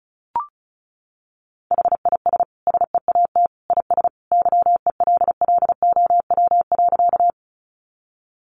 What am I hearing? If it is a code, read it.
Morse code, heard as a keyed tone, about 34 wpm, 700 Hz. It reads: HIS SEAT IS YELLOW.
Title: Additional Easy Phrases 35wpm